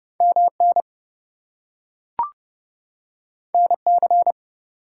Morse code, heard as a keyed tone, about 30 wpm, 700 Hz. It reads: MN NC
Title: US States and Canadian Provinces 30wpm